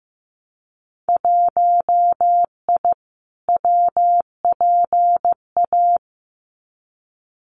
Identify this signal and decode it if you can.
Morse code, heard as a keyed tone, about 15 wpm, 700 Hz. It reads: 1I WPA
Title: Winter Field Day Exchanges Encore 15wpm